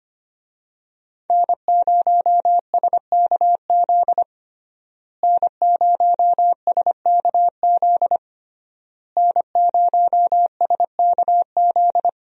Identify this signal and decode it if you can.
Morse code, heard as a keyed tone, about 25 wpm, 700 Hz. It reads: N0SKZ N0SKZ N0SKZ
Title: USA Call Signs - 5 Characters - Repeated 3x - 25wpm